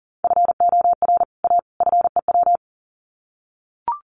Morse code, heard as a keyed tone, about 40 wpm, 700 Hz. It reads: FOR A FEW
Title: Sets of 3 Words 40wpm